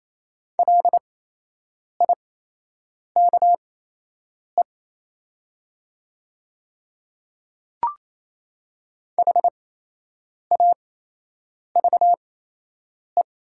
Morse code, heard as a keyed tone, about 28 wpm, spaced as for 8 wpm, 700 Hz. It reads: LIKE HAVE
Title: 8x Character Spacing - Top 100 Words - 28wpm